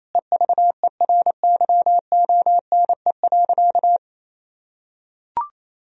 Morse code, heard as a keyed tone, about 28 wpm, 700 Hz. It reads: EVERYONE.
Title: Sentences from Top 1000 Words - Review Part 4 - 28wpm